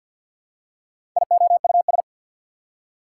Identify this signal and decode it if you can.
Morse code, heard as a keyed tone, about 54 wpm, 690 Hz. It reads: IOUS
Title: Prefixes and Suffixes 50wpm